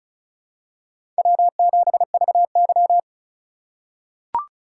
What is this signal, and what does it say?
Morse code, heard as a keyed tone, about 36 wpm, 700 Hz. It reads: W7VY
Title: USA Call Signs - 3x Character Spacing - 35wpm